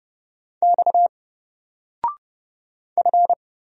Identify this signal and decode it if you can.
Morse code, heard as a keyed tone, about 30 wpm, 700 Hz. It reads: X F
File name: Single-Letter-Number-30wpm